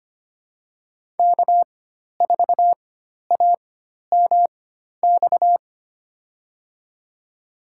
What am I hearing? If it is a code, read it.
Morse code, heard as a keyed tone, about 25 wpm, spaced as for 12 wpm, 700 Hz. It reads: K4AMX
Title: USA Call Signs - 5 Characters - 4x Character Spacing - 25wpm